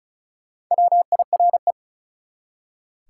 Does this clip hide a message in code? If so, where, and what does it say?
Yes: Morse code from 0.7 to 1.7 s, heard as a keyed tone, about 46 wpm, 700 Hz. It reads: WIRE